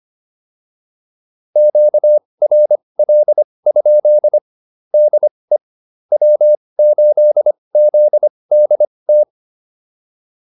Morse code, heard as a keyed tone, about 25 wpm, spaced as for 20 wpm, 600 Hz. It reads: QRL? DE W8ZDT